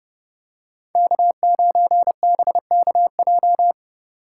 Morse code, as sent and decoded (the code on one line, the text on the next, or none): -.- ----. -... -.- .---
K9BKJ